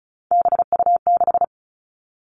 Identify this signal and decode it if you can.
Morse code, heard as a keyed tone, about 36 wpm, 700 Hz. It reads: BU6
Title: Three Challenging Letter-Number Combinations 35wpm